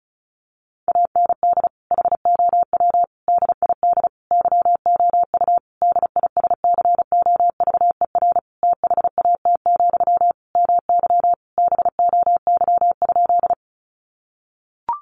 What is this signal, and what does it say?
Morse code, heard as a keyed tone, about 34 wpm, 700 Hz. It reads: AND HOW DID YOU DISCOVER THAT, MY BOY?
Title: Sentences from Top 1100 Words 35wpm